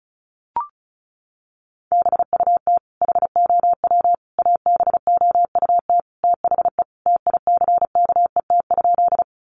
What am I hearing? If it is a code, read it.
Morse code, heard as a keyed tone, about 34 wpm, 700 Hz. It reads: BUT HOW ABOUT THE TICKET?
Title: Sentences from Top 1400 Words 35wpm